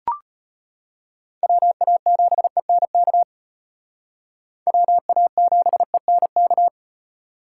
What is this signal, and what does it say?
Morse code, heard as a keyed tone, about 32 wpm, 700 Hz. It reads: WA7ENK WA7ENK